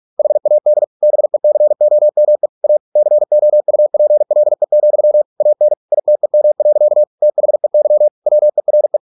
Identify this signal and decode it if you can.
Morse code, heard as a keyed tone, about 46 wpm, 600 Hz. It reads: HAD BECOME A COUPLE, AN ITEM. THEY WERE